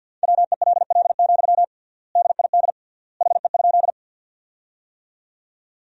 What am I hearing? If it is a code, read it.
Morse code, heard as a keyed tone, about 53 wpm, 700 Hz. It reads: WELL, DID HE?
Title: 3x Word-Spacing - Sentences from Top 200 Words - 50wpm